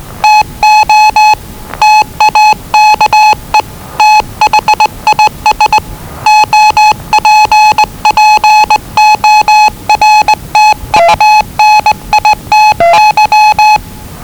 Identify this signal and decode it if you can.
Morse code, heard as a keyed tone, about 20 wpm, 850 Hz. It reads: TO TAKE THIS OPPORTUNITY
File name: WellingtonRadioZlwCloseDownOnHfAndMf / 01-Zlw 4277khz 417.5 qrt until 0925z